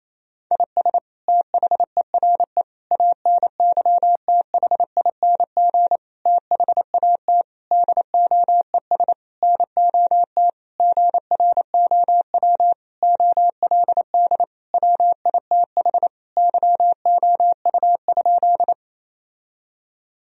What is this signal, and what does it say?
Morse code, heard as a keyed tone, about 28 wpm, 700 Hz. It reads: IS THERE ANYTHING THAT DOES NOT GROW OLD WITH YOU?
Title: Sentences from Top 1000 Words - Review Part 1 - 28wpm